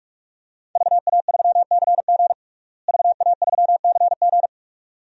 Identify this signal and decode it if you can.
Morse code, heard as a keyed tone, about 43 wpm, 700 Hz. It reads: VA3CG VA3CG
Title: DXCC-100 Call Signs - 5 Characters - Repeated 2x - 45wpm